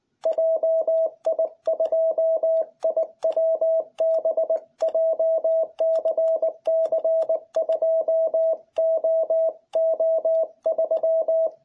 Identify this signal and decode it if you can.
Morse code, heard as a keyed tone, about 19 wpm, 660 Hz. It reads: JI2IWB JCC2OO3